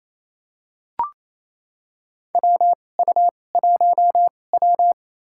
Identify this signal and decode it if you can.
Morse code, heard as a keyed tone, about 28 wpm, spaced as for 20 wpm, 700 Hz. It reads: WU1W
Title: USA Call Signs - 4 Characters - 2x Character Spacing - 28wpm